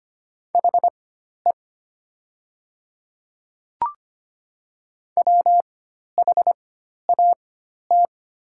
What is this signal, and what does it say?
Morse code, heard as a keyed tone, about 25 wpm, spaced as for 12 wpm, 700 Hz. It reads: HE WHAT
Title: 4x Character Spacing - Top 200 Words - 25wpm